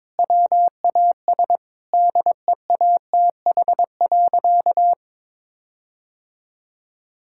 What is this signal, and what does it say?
Morse code, heard as a keyed tone, about 22 wpm, 700 Hz. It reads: WAS DEATH.
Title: Sentences from Top 1600 Words 22wpm